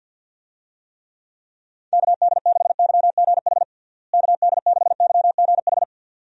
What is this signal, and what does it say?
Morse code, heard as a keyed tone, about 52 wpm, 700 Hz. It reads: KD6YGH KD6YGH
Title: USA Call Signs - 6 Characters - Repeated 2x - 50wpm